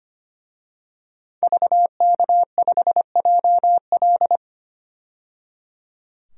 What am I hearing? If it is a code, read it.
Morse code, heard as a keyed tone, about 25 wpm, 700 Hz. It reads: VK5JL
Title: DXCC-100 Call Signs - 5 Characters - 25wpm